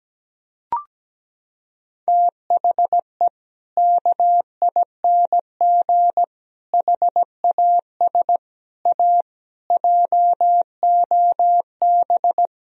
Morse code, heard as a keyed tone, about 17 wpm, 700 Hz. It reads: THE KING HAS A JOB
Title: Sentences from Top 1000 Words - Review Part 3 - 17wpm